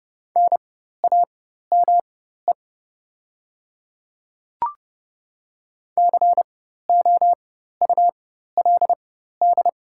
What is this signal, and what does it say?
Morse code, heard as a keyed tone, about 30 wpm, spaced as for 14 wpm, 690 Hz. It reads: NAME COULD
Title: 4x Character Spacing - Top 200 Words - 30wpm